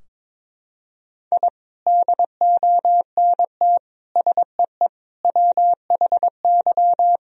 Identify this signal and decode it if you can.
Morse code, heard as a keyed tone, about 22 wpm, 700 Hz. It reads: I DONT SEE WHY
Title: Sentences from Top 1000 Words - Review Part 2 - 22wpm